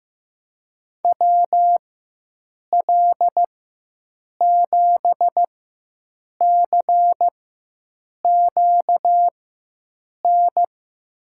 Morse code, heard as a keyed tone, about 15 wpm, spaced as for 7 wpm, 700 Hz. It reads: WL7CQN
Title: USA Call Signs - 6 Characters - 4x Character Spacing - 15wpm